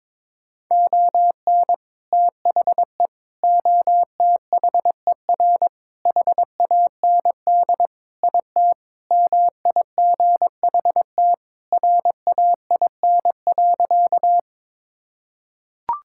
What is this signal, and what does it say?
Morse code, heard as a keyed tone, about 21 wpm, 700 Hz. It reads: ON THE OTHER HAND IT MIGHT RAIN.
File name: practice / 22wpm / Sentences with 3-Word Phrases - 22wpm